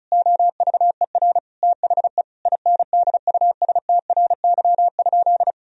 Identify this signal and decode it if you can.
Morse code, heard as a keyed tone, about 34 wpm, 700 Hz. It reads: OVER THE INDUSTRY?
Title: Sentences from Top 1000 Words - Review Part 4 - 35wpm